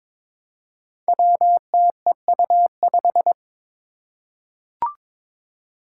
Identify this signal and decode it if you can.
Morse code, heard as a keyed tone, about 22 wpm, 700 Hz. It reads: WTEU5
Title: Five Challenging Letter-Number Combinations 22wpm